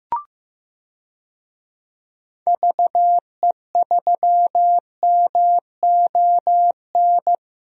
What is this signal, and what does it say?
Morse code, heard as a keyed tone, about 16 wpm, 700 Hz. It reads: VE3MON
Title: DXCC-100 Call Signs - 6 Characters - Repeated 3x - 15wpm